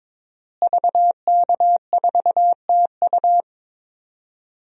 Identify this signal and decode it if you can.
Morse code, heard as a keyed tone, about 22 wpm, 700 Hz. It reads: VK4TU
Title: DXCC-100 Call Signs - 5 Characters - Repeated 3x - 22wpm